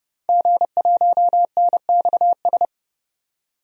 Morse code, heard as a keyed tone, about 31 wpm, 700 Hz. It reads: G1NXS
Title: DXCC-100 Call Signs - 5 Characters - 30wpm